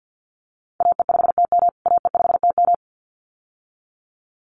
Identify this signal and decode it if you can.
Morse code, heard as a keyed tone, about 50 wpm, 700 Hz. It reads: AE5TM AE5TM
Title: USA Call Signs Repeated 2x 50wpm